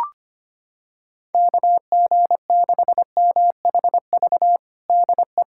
Morse code, heard as a keyed tone, about 25 wpm, 700 Hz. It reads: KG6MHV DE
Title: USA Callsign DE Callsign 25wpm